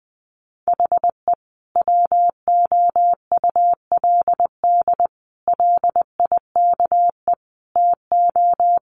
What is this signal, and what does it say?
Morse code, heard as a keyed tone, about 20 wpm, 700 Hz. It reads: HE WOULD LIKE TO
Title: Sentences from Top 1300 Words 20wpm